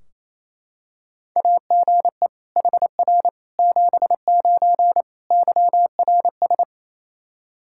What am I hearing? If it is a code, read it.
Morse code, heard as a keyed tone, about 28 wpm, 700 Hz. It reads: AGE HR 79 YRS
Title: Practice QSO - Exchange 3 - More Info - 28wpm